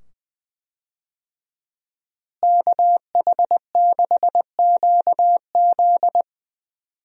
Morse code, heard as a keyed tone, about 20 wpm, 700 Hz. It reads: KH6QZ
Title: USA Call Signs - 5 Characters - Repeated 2x - 20wpm